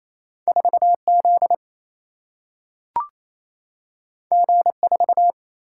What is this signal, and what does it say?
Morse code, heard as a keyed tone, about 28 wpm, 700 Hz. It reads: 4Z G4
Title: Single-Letter Single-Number 28wpm